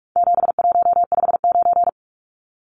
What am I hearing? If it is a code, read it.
Morse code, heard as a keyed tone, about 44 wpm, 700 Hz. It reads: 7159
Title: Numbers - 4-Digit 45wpm